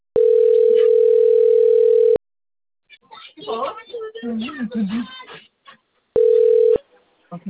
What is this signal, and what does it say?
Ringback tone (North America), 440 + 480 Hz, with speech.
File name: out- - - - - .72290